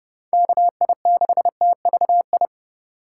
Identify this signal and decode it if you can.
Morse code, heard as a keyed tone, about 30 wpm, 700 Hz. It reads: KI6TVI
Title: USA Call Signs - 6 Characters - 3x Character Spacing - 30wpm